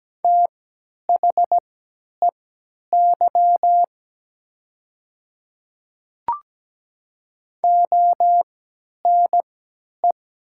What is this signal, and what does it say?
Morse code, heard as a keyed tone, about 17 wpm, spaced as for 10 wpm, 700 Hz. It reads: THEY ONE
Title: 3x Character Spacing - Top 100 Words - 17wpm